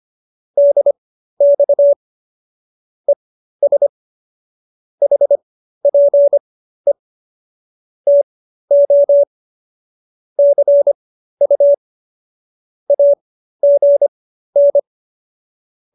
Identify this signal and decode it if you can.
Morse code, heard as a keyed tone, about 25 wpm, spaced as for 13 wpm, 570 Hz. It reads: DX ES HPE TO CU AGN